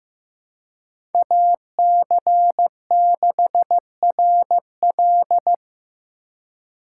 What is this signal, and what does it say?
Morse code, heard as a keyed tone, about 15 wpm, 700 Hz. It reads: AC6RL
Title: USA Call Signs - 5 Characters - 3x Character Spacing - 15wpm